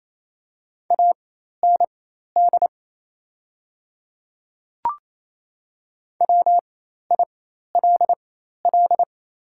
Morse code, heard as a keyed tone, about 28 wpm, spaced as for 13 wpm, 700 Hz. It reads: AND WILL
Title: 4x Character Spacing - Top 100 Words - 28wpm